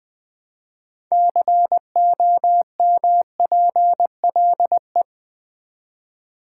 Morse code, heard as a keyed tone, about 20 wpm, 700 Hz. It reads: COMPLE
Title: Build A Word - 3 Letters At A Time - 8-Letter Words - 20wpm